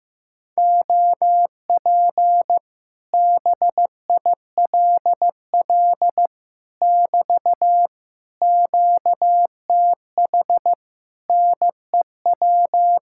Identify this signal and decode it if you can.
Morse code, heard as a keyed tone, about 15 wpm, 700 Hz. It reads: OP BILL <BT> QTH NEW
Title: Speed-Racing Common QSO Phrases Encore - 15wpm